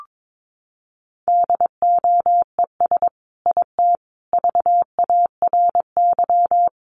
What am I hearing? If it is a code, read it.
Morse code, heard as a keyed tone, about 22 wpm, 700 Hz. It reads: DOES IT VARY
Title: Sentences from Top 1200 Words 22wpm